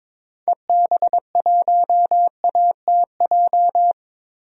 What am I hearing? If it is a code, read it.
Morse code, heard as a keyed tone, about 22 wpm, 700 Hz. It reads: EB1ATJ
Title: DXCC-100 Call Signs - 6 Characters - Repeated 2x - 22wpm